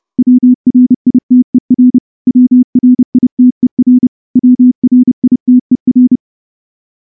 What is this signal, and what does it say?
Morse code, heard as a keyed tone, about 29 wpm, 250 Hz. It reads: WRITER WRITER WRITER
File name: morse-training-30WPM